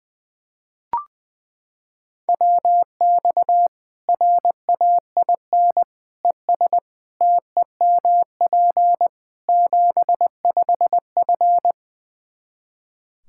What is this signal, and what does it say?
Morse code, heard as a keyed tone, about 20 wpm, 700 Hz. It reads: WX RAIN ES TEMP 75F
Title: Practice QSO - Exchange 2 - Info - 20wpm